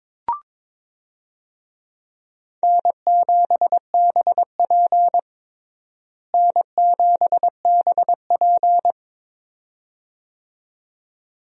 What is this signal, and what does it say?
Morse code, heard as a keyed tone, about 22 wpm, 700 Hz. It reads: N7BP N7BP
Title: USA Call Signs - 4 Characters - Repeated 2x - 22wpm